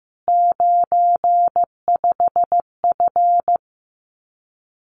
Morse code, heard as a keyed tone, about 15 wpm, 700 Hz. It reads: 95F
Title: Common QSO Phrases Encore 15wpm